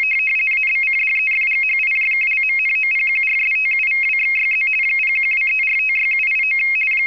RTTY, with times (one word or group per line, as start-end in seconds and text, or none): 0.5-4.7 ABCDEFGHIJKLMNOPQRSTUVWXYZ
5.0-6.6 0123456789